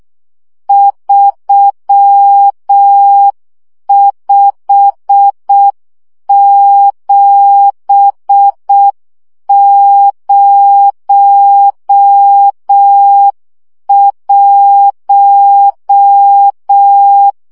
Morse code, heard as a keyed tone, about 6 wpm, 800 Hz. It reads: 35701
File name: morze vizsga 2011-1 5